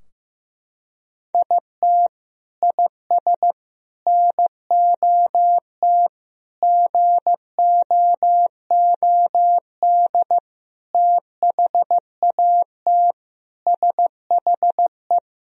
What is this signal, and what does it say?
Morse code, heard as a keyed tone, about 15 wpm, 700 Hz. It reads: IT IS NOT GOOD THAT SHE
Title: Sentences from Top 1000 Words - Review Part 4 - 15wpm